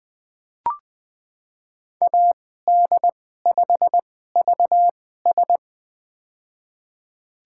Morse code, heard as a keyed tone, about 20 wpm, spaced as for 14 wpm, 700 Hz. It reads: AD5VS
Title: USA Call Signs - 2x Character Spacing - 20wpm